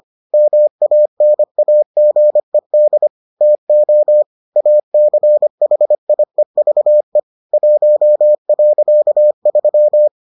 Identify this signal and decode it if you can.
Morse code, heard as a keyed tone, about 25 wpm, 600 Hz. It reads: MANAGED TO ACHIEVE 1.3